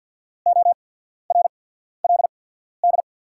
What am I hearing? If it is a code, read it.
Morse code, heard as a keyed tone, about 48 wpm, spaced as for 14 wpm, 700 Hz. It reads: ORLD
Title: 8x Character Spacing - Top 300 Words - 50wpm